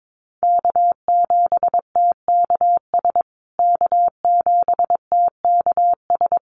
Morse code, heard as a keyed tone, about 22 wpm, 700 Hz. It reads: K7TKS K7TKS
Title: USA Call Signs Repeated 3x 22wpm